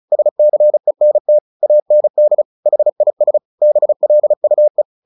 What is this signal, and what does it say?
Morse code, heard as a keyed tone, about 35 wpm, 600 Hz. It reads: SCENT AND HIS BLUE
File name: the old man and the sea 2 35wpm